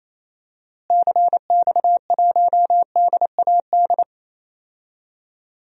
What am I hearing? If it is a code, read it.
Morse code, heard as a keyed tone, about 28 wpm, 700 Hz. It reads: CX1DAD
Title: DXCC-100 Call Signs - 6 Characters - Repeated 2x - 28wpm